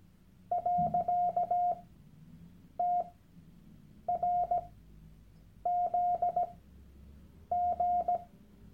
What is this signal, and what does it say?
Morse code, heard as a keyed tone, about 17 wpm, spaced as for 7 wpm, 710 Hz. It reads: .TRZG
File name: Workshop Dateien / CW-Online-Kurs / Lektion 001